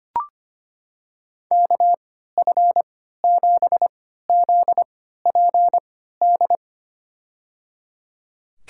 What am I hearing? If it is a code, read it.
Morse code, heard as a keyed tone, about 25 wpm, spaced as for 14 wpm, 690 Hz. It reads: KF7ZPD
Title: USA Call Signs - 6 Characters - 3x Character Spacing - 25wpm